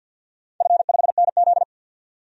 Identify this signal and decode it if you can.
Morse code, heard as a keyed tone, about 51 wpm, 700 Hz. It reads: F5NZ
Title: DXCC-100 Call Signs - 4 Characters - 50wpm